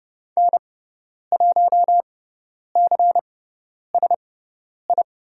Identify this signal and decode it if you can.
Morse code, heard as a keyed tone, about 30 wpm, spaced as for 10 wpm, 700 Hz. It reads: N1CSI
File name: Lesson 18g - Letter m - Call Signs 30wpm